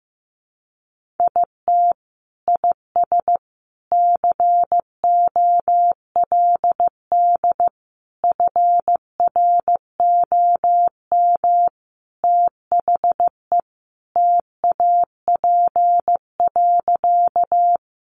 Morse code, heard as a keyed tone, about 15 wpm, 700 Hz. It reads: IT IS COLD FROM THE TAP.
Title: Sentences from Top 1400 Words 15wpm